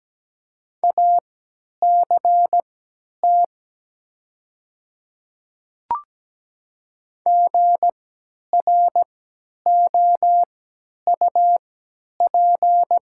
Morse code, heard as a keyed tone, about 17 wpm, spaced as for 10 wpm, 700 Hz. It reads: ACT GROUP
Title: 3x Character Spacing - Top 300 Words - 17wpm